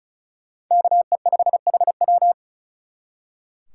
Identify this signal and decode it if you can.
Morse code, heard as a keyed tone, about 46 wpm, 700 Hz. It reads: KE5HW